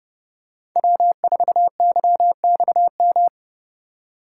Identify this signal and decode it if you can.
Morse code, heard as a keyed tone, about 29 wpm, 700 Hz. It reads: W4YXM